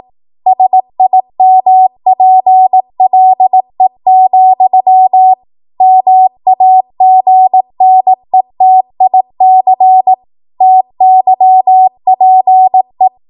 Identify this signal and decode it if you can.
Morse code, heard as a keyed tone, about 18 wpm, 750 Hz. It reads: SIMPLE, MAGNETIC TYPE
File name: Morse / 18WPM